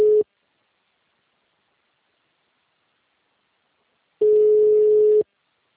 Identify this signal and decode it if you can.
Ringback tone (Europe, 425 Hz).
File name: out- -2010- - - .63249